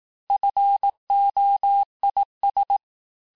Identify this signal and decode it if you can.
Morse code, heard as a keyed tone, about 18 wpm, 800 Hz. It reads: FOIS
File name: cours-du-lundi files / cours-1852-texte-clair-vitesse-17MPM-puis-22MPM